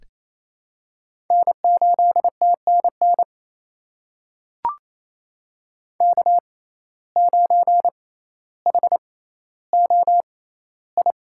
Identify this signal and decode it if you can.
Morse code, heard as a keyed tone, about 27 wpm, spaced as for 14 wpm, 700 Hz. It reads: N8TNN K9HOI
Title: USA Call Signs - 5 Characters - 6x Character Spacing - 28wpm